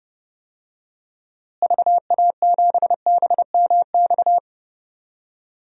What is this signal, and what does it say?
Morse code, heard as a keyed tone, about 30 wpm, 700 Hz. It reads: VA7BMX
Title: DXCC-100 Call Signs - 6 Characters - Repeated 3x - 30wpm